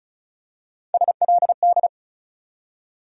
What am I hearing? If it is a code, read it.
Morse code, heard as a keyed tone, about 36 wpm, 700 Hz. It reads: SLD
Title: USA Call Sign Postfixes - 3 Letters 35wpm